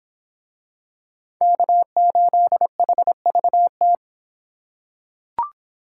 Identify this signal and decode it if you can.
Morse code, heard as a keyed tone, about 24 wpm, 700 Hz. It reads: K8HVT